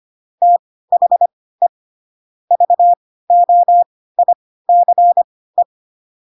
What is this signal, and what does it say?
Morse code, heard as a keyed tone, about 25 wpm, spaced as for 16 wpm, 700 Hz. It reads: THE VOICE
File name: Pinocchio / Pinocchio ch6 10 25 ik1hge